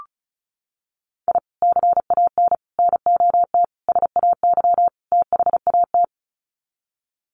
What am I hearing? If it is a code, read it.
Morse code, heard as a keyed tone, about 34 wpm, 700 Hz. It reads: I CAN NOT SAY THAT
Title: Sets of 5 Words Encore2 35wpm